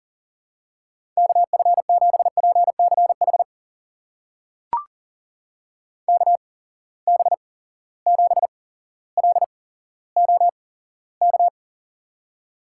Morse code, heard as a keyed tone, about 40 wpm, spaced as for 17 wpm, 700 Hz. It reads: KF7PCH KB7LOK
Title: USA Call Signs - 6 Characters - 8x Character Spacing - 40wpm